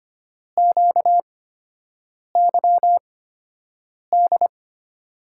Morse code, heard as a keyed tone, about 25 wpm, spaced as for 7 wpm, 700 Hz. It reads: QYD
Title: USA Call Signs - 6 Characters - 8x Character Spacing - 25wpm